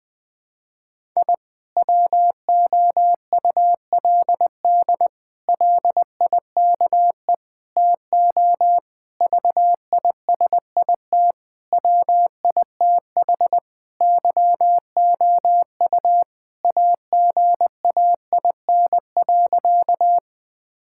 Morse code, heard as a keyed tone, about 20 wpm, 700 Hz. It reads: I WOULD LIKE TO VISIT WITH YOU AGAIN.
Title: Sentences from Top 1000 Words - Review Part 6 - 20wpm